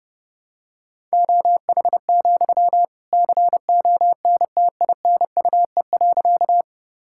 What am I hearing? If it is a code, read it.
Morse code, heard as a keyed tone, about 30 wpm, 700 Hz. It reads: OH, CONTINUE.
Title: Sentences from Top 1000 Words - Review Part 6 - 30wpm